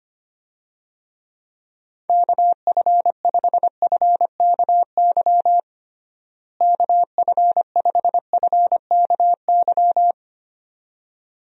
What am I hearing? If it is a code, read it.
Morse code, heard as a keyed tone, about 25 wpm, 700 Hz. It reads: KF5FKY KF5FKY
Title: USA Call Signs - 6 Characters - Repeated 2x - 25wpm